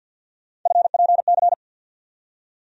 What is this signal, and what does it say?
Morse code, heard as a keyed tone, about 47 wpm, 700 Hz. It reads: FPC
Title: USA Call Sign Postfixes - 3 Letters 50wpm